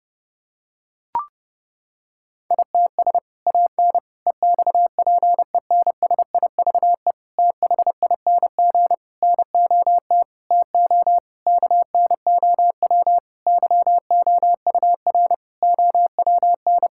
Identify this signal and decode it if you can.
Morse code, heard as a keyed tone, about 30 wpm, 700 Hz. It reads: ITS AN EXPENSIVE THING NOT TO KNOW YOUR OWN
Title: Sentences from Top 1000 Words - Review Part 6 - 30wpm